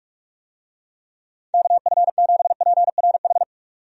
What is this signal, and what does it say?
Morse code, heard as a keyed tone, about 42 wpm, 700 Hz. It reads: KF7PRH
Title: USA Call Signs - 6 Characters - 45wpm